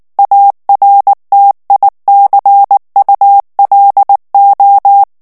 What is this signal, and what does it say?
Morse code, heard as a keyed tone, about 19 wpm, 800 Hz. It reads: ARTICULO
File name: morse mp3 / ESPAGNOL TEXTE EN CLAIR 2 VITESSE 18 MOTS MINUTES